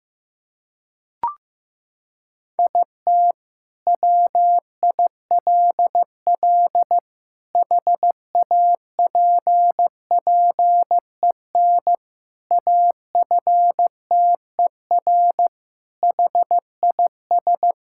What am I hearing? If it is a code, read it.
Morse code, heard as a keyed tone, about 15 wpm, 700 Hz. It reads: IT WILL HAPPEN AFTER HIS
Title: Sentences from Top 1000 Words - Review Part 5 - 15wpm